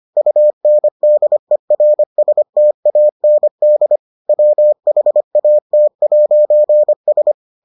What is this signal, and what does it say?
Morse code, heard as a keyed tone, about 25 wpm, 600 Hz. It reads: UNDERSTAND WHAT'S